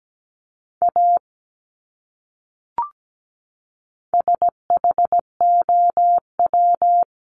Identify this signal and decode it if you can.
Morse code, heard as a keyed tone, about 17 wpm, 700 Hz. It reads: A SHOW
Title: Top 300 Words 17wpm